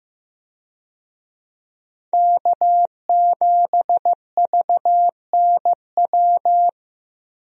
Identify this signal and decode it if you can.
Morse code, heard as a keyed tone, about 15 wpm, 700 Hz. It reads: K7VNW
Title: USA Call Signs - 5 Characters - Repeated 2x - 15wpm